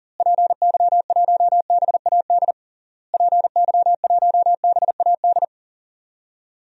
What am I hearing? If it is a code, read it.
Morse code, heard as a keyed tone, about 37 wpm, 700 Hz. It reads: PY1BAD PY1BAD
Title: DXCC-100 Call Signs - 6 Characters - Repeated 3x - 40wpm